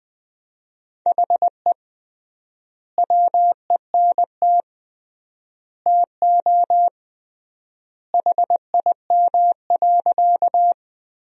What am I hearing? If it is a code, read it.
Morse code, heard as a keyed tone, about 20 wpm, 700 Hz. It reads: HE WENT TO HIM.
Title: 3x Word-Spacing - Sentences from Top 200 Words - 20wpm